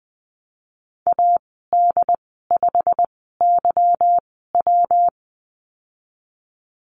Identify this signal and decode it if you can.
Morse code, heard as a keyed tone, about 20 wpm, spaced as for 15 wpm, 700 Hz. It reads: AD5YW
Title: 147 Lesson 24g - Letter y - Call Signs 20wpm